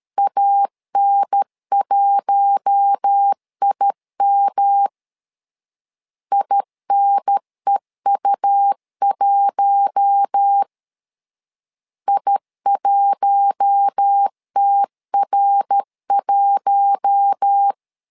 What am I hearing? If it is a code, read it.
Morse code, heard as a keyed tone, about 13 wpm, 800 Hz. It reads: AN1IM INEU1 I1TR1
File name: LEZIONI / 4 Q1 12wpm 2028